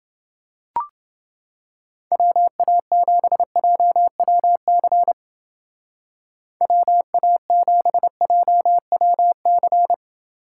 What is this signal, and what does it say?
Morse code, heard as a keyed tone, about 29 wpm, 700 Hz. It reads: WA7JWC WA7JWC